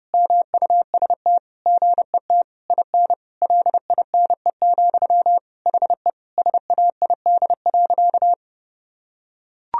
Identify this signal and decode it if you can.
Morse code, heard as a keyed tone, about 32 wpm, 700 Hz. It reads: MUST GET IN LINE, HE SAID.
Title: Sentences from Top 1000 Words - Review Part 1 - 30wpm